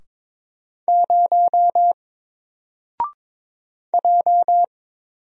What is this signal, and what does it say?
Morse code, heard as a keyed tone, about 22 wpm, 700 Hz. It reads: 0 J
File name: Single-Letter-Number-22wpm